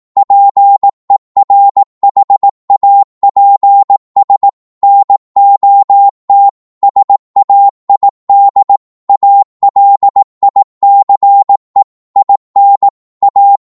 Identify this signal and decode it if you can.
Morse code, heard as a keyed tone, about 18 wpm, 800 Hz. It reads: PERHAPS NOT SAID ALICE IN A
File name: Morse / psg-18wpm